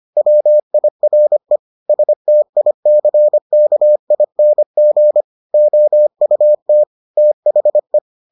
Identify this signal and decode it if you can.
Morse code, heard as a keyed tone, about 25 wpm, 600 Hz. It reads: WIRE STICKING OUT THE